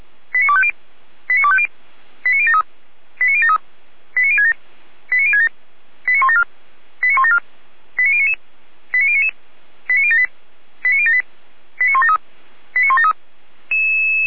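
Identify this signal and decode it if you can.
Five-tone selective call (ZVEI), heard as quick runs of five tones, tones of about 70 ms. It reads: 89250 89250 89073 89073 89067 89067 89174 89174 89099 89099 89078 89078 89173 89173
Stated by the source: Five-tone ZVEI selective call with code 89246, received by SDR.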